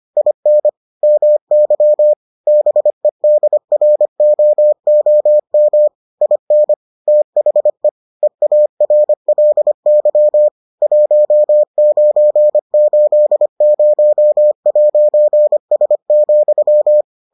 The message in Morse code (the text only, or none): IN MY BEDROOM IN THE EARLY 1980'S,